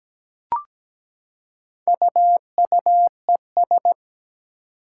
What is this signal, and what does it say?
Morse code, heard as a keyed tone, about 17 wpm, 700 Hz. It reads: UUES
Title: Four Challenging Letter-Number Combinations 17wpm